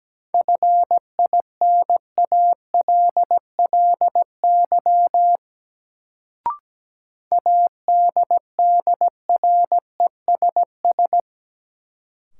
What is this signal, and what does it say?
Morse code, heard as a keyed tone, about 17 wpm, 700 Hz. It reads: FINALLY ADDRESS
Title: Sentences from Top 1000 Words - New Word Drill - 17wpm